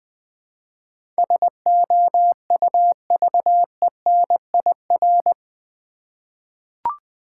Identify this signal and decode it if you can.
Morse code, heard as a keyed tone, about 20 wpm, 690 Hz. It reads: SOUVENIR
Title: Easy Words with Hints - 20wpm